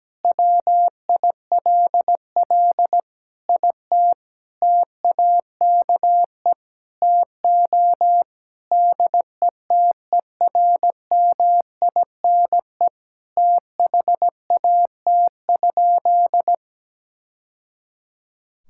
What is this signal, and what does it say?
Morse code, heard as a keyed tone, about 17 wpm, 700 Hz. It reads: WILL IT TAKE TO DETERMINE THAT?
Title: Sentences from Top 1000 Words - Review Part 5 - 17wpm